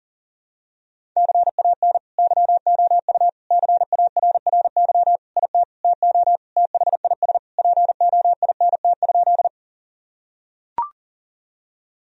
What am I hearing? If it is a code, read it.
Morse code, heard as a keyed tone, about 40 wpm, 700 Hz. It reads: CAN YOU CARRY IT TO THIS POINT?
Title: Sentences from Top 1000 Words - Review Part 1 - 40wpm